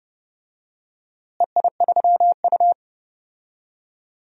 Morse code, heard as a keyed tone, about 30 wpm, 700 Hz. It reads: EI3U
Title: DXCC-100 Call Signs - 4 Characters - Repeated 3x - 30wpm